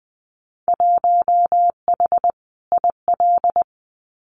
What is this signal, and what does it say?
Morse code, heard as a keyed tone, about 20 wpm, 700 Hz. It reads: 1H IL
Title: Winter Field Day Exchanges 20wpm